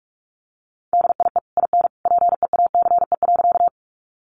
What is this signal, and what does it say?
Morse code, heard as a keyed tone, about 44 wpm, 700 Hz. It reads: DIE IN PEACE.
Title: Sentences from Top 1500 Words 45wpm